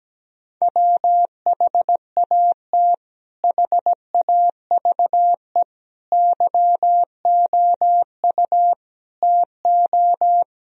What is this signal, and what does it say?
Morse code, heard as a keyed tone, about 17 wpm, 700 Hz. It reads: WHAT HAVE YOU TO
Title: Sentences from Top 1000 Words - Review Part 2 - 17wpm